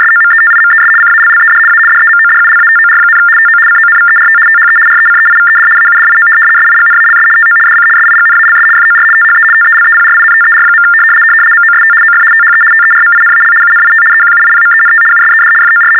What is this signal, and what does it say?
RTTY, about 75 baud. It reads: SOMETIMES CALLED A DEMODULATOR, TO CONVERT THE RECEIVED AUDIO SIGNALS TO DC SIGNALS FOR THE TELEPRINTER. MOST OF THE TERMINAL UNIT EQUIPMENT USED FOR
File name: teletype-RTTY-75